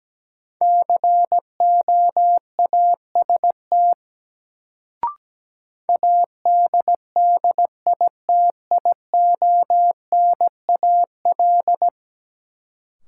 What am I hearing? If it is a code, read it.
Morse code, heard as a keyed tone, about 17 wpm, 700 Hz. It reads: COAST ADDITIONAL
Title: Sentences from Top 1000 Words - Review Part 4 - New Word Drill - 17wpm